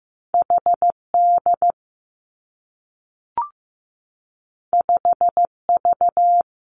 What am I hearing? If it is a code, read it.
Morse code, heard as a keyed tone, about 15 wpm, 700 Hz. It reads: H D 5 V
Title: Two Challenging Letter-Number Combinations 15wpm